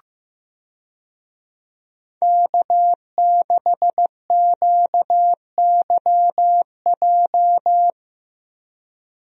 Morse code, heard as a keyed tone, about 15 wpm, 700 Hz. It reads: K6QYJ
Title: USA Call Signs - 5 Characters - Repeated 3x - 15wpm